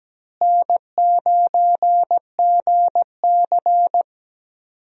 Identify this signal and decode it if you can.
Morse code, heard as a keyed tone, about 17 wpm, 700 Hz. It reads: N9GC
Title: USA Call Signs - 4 Characters - Repeated 3x - 17wpm